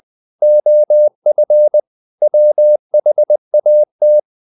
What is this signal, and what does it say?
Morse code, heard as a keyed tone, about 20 wpm, 600 Hz. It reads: OF WHAT
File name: the old man and the sea / the old man and the sea part 1 20wpm